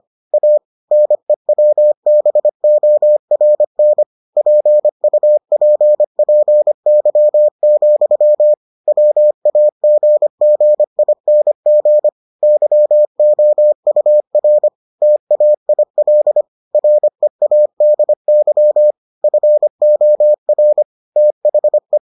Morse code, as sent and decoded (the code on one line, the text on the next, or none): .- / -. .-- -... --- .-. -. / .--. ..- .--. .--. -.-- --..-- / .-- .- --. --. .. -. --. / -.-- --- ..- .-. / - .- .. .-.. / .-. .- -.. -.-- / ..-. --- .-. / - .... .
A NEWBORN PUPPY, WAGGING YOUR TAIL READY FOR THE